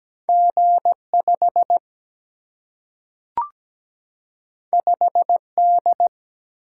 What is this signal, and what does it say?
Morse code, heard as a keyed tone, about 17 wpm, 700 Hz. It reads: G5 5D
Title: Single-Letter Single-Number 17wpm